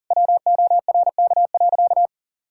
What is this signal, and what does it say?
Morse code, heard as a keyed tone, about 40 wpm, 700 Hz. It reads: WORK.
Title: Sentences from Top 1000 Words - Review Part 6 - 40wpm